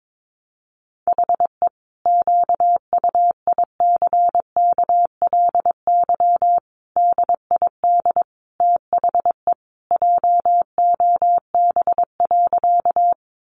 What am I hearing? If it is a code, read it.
Morse code, heard as a keyed tone, about 22 wpm, 700 Hz. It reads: HE QUICKLY DID THE JOB.
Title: Sentences from Top 700 Words 22wpm